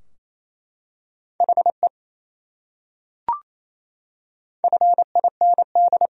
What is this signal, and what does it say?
Morse code, heard as a keyed tone, about 28 wpm, 700 Hz. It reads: HE FIND
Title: Top 100 Words Repeated 28wpm